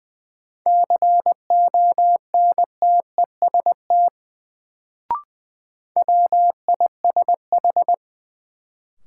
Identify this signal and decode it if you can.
Morse code, heard as a keyed tone, about 20 wpm, 700 Hz. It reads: CONTEST WISH
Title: Sentences from Top 1600 Words - New Word Drill - 20wpm